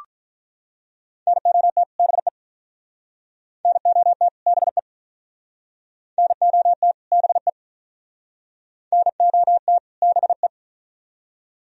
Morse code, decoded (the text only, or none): NOT BE NOT BE NOT BE NOT BE